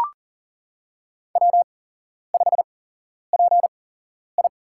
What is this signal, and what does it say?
Morse code, heard as a keyed tone, about 40 wpm, spaced as for 11 wpm, 700 Hz. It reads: W5PI
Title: USA Call Signs - 4 Characters - 8x Character Spacing - 40wpm